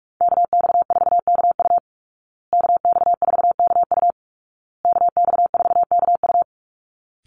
Morse code, heard as a keyed tone, about 44 wpm, 700 Hz. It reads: KX4KU KX4KU KX4KU
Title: Call Signs Repeated 2x - 45wpm